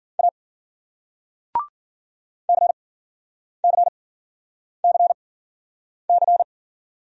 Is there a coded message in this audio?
Yes: Morse code from 0.2 to 6.4 s, heard as a keyed tone, about 53 wpm, spaced as for 9 wpm, 700 Hz. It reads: A CCCC